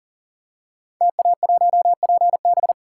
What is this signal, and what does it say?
Morse code, heard as a keyed tone, about 38 wpm, 700 Hz. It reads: TA1PB
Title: Code Groups - 5 Letter:Numbers - 40wpm